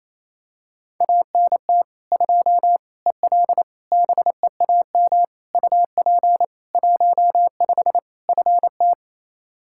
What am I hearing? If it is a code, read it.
Morse code, heard as a keyed tone, about 27 wpm, 700 Hz. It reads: ANT 2 EL BEAM UP 15 FT
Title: Practice QSO - Exchange 2 - Info - 28wpm